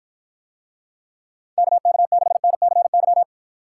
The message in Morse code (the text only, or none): KK6NCY